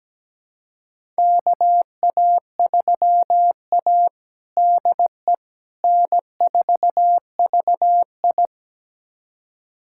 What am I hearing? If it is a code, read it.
Morse code, heard as a keyed tone, about 17 wpm, 700 Hz. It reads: KA3A DE N4VI
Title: USA Callsign DE Callsign - 4 Characters - 17wpm